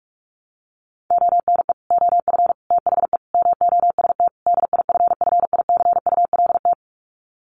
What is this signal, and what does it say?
Morse code, heard as a keyed tone, about 44 wpm, 700 Hz. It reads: ONE OF THE MOST DIFFICULT
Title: Sets of 5 Words Encore2 45wpm